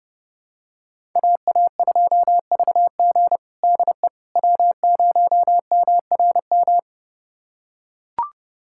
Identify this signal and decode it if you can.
Morse code, heard as a keyed tone, about 30 wpm, 700 Hz. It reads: AA2VG DE W0MRM
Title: USA Callsign DE Callsign - 5 Characters - 30wpm